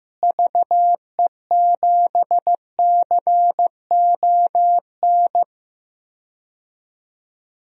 Morse code, heard as a keyed tone, about 15 wpm, 700 Hz. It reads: VE7CON
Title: DXCC-100 Call Signs - 6 Characters - Repeated 2x - 15wpm